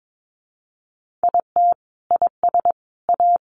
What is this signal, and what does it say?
Morse code, heard as a keyed tone, about 22 wpm, 700 Hz. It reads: IT IS A
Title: Sentences from Top 1500 Words 22wpm